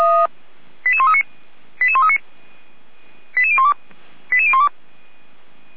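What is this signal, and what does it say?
Five-tone selective call (ZVEI), heard as quick runs of five tones, tones of about 70 ms. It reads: 88139 88139 88012 88012
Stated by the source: ZVEI five-tone selective call received via SDR, from the Feuerwehr Orlamünde audio server.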